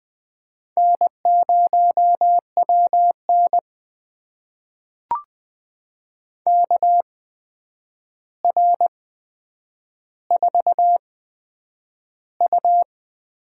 Morse code, heard as a keyed tone, about 20 wpm, spaced as for 9 wpm, 700 Hz. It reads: N0WN KR4U
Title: USA Call Signs - 4 Characters - 8x Character Spacing - 20wpm